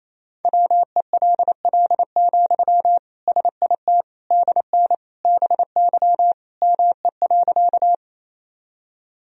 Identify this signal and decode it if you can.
Morse code, heard as a keyed tone, about 28 wpm, 700 Hz. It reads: WELL, SIT DN BY ME.
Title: Sentences from Top 500 Words and Abbreviations 28wpm